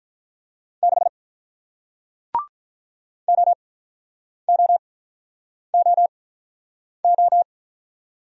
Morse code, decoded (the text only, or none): 6 OOOO